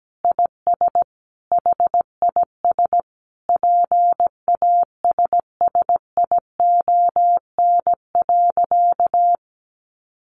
Morse code, heard as a keyed tone, about 17 wpm, 700 Hz. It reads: IS HIS PASSION.
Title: Sentences from Top 1300 Words 17wpm